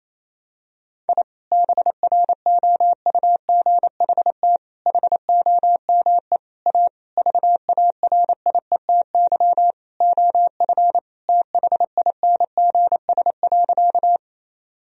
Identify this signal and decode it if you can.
Morse code, heard as a keyed tone, about 28 wpm, 700 Hz. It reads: I BROUGHT HOME A VARIETY OF THINGS.
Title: Sentences from Top 1000 Words - Review Part 4 - 28wpm